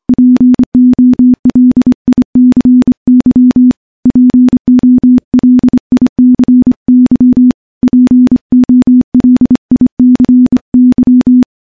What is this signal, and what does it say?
Morse code, heard as a keyed tone, about 25 wpm, 250 Hz. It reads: POLICY POLICY POLICY
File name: morse-training-25WPM